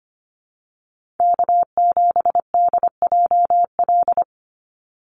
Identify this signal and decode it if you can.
Morse code, heard as a keyed tone, about 25 wpm, 700 Hz. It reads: K7DJL
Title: NA SOTA Chaser Callsigns 25wpm